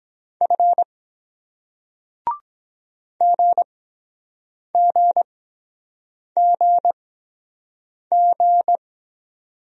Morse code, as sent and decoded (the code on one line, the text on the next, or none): ..-. / --. --. --. --.
F GGGG